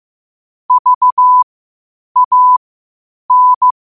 Morse code, heard as a keyed tone, about 15 wpm, spaced as for 8 wpm, 1000 Hz. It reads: VAN